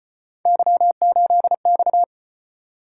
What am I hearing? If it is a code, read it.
Morse code, heard as a keyed tone, about 34 wpm, 700 Hz. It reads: Y8X